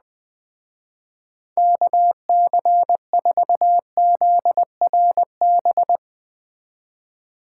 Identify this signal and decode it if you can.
Morse code, heard as a keyed tone, about 20 wpm, 700 Hz. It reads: KC4ZRB
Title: Speed Racing - USA 2x3 Call Signs 17wpm